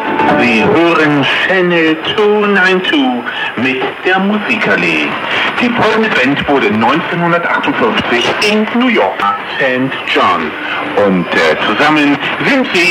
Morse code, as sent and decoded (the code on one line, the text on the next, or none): none